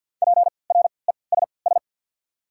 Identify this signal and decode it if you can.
Morse code, heard as a keyed tone, about 53 wpm, spaced as for 29 wpm, 690 Hz. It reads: PRESS
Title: 3x Character Spacing - Top 300 Words - 50wpm